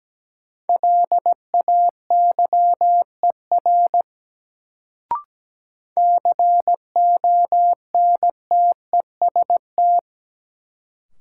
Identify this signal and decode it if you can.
Morse code, heard as a keyed tone, about 17 wpm, 700 Hz. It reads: LAYER CONTEST
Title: Sentences from Top 1600 Words - New Word Drill Alt - 17wpm